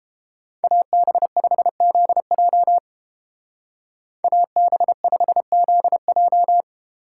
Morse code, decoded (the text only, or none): AB5ZJ AB5ZJ